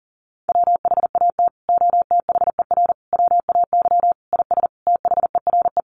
Morse code, heard as a keyed tone, about 40 wpm, 700 Hz. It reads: WHAT OTHER WAY IS THERE
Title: Sentences from Top 100 Words 40wpm